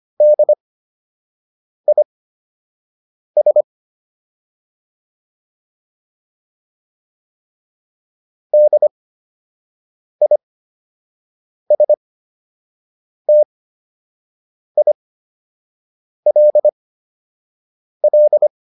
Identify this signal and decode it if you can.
Morse code, heard as a keyed tone, about 25 wpm, spaced as for 6 wpm, 600 Hz. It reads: DIS DISTILL